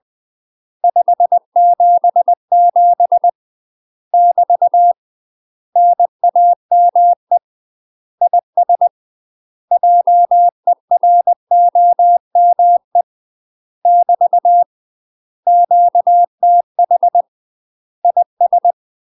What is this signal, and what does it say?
Morse code, heard as a keyed tone, about 20 wpm, 700 Hz. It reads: 577 <BT> NAME IS JEROME <BT> QTH IS
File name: UK MASTER QSOs TRAINING / 20WPM / UK MASTER QSOs 0043